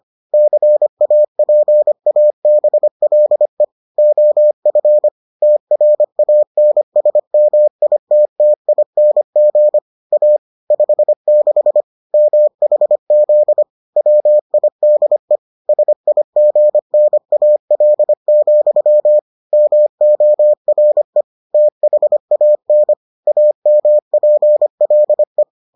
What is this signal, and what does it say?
Morse code, heard as a keyed tone, about 25 wpm, 600 Hz. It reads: CAPABLE OF TRANSMITTING A 56 MHZ WIDE SIGNAL, MORE THAN AMPLE